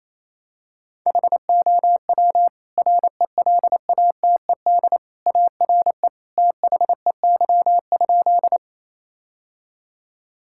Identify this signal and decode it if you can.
Morse code, heard as a keyed tone, about 28 wpm, 700 Hz. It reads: HOW RELATED ARE THEY?
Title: Sentences from Top 1000 Words - Review Part 5 - 28wpm